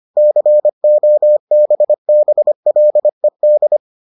Morse code, heard as a keyed tone, about 25 wpm, 600 Hz. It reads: COBBLED